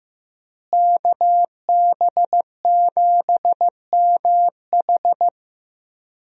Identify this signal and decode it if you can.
Morse code, heard as a keyed tone, about 15 wpm, 700 Hz. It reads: KB7MH
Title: USA Call Signs - 5 Characters - Repeated 3x - 15wpm